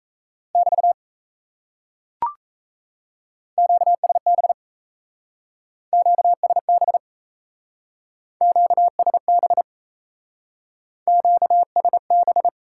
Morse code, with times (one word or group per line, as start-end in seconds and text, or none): 0.6-0.9 <BT>
3.6-4.5 QSB
5.9-7.0 QSB
8.4-9.6 QSB
11.1-12.5 QSB